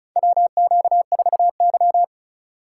Morse code, heard as a keyed tone, about 36 wpm, 700 Hz. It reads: WQ4Y
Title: USA Call Signs - 4 Characters - 3x Character Spacing - 35wpm